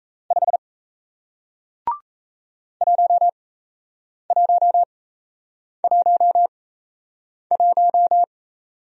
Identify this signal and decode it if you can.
Morse code, heard as a keyed tone, about 37 wpm, spaced as for 8 wpm, 700 Hz. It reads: 5 1111